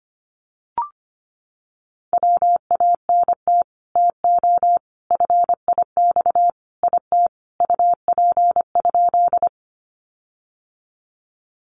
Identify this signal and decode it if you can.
Morse code, heard as a keyed tone, about 25 wpm, 700 Hz. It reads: WANT TO FIX IT UP?
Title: Sentences from Top 1400 Words 25wpm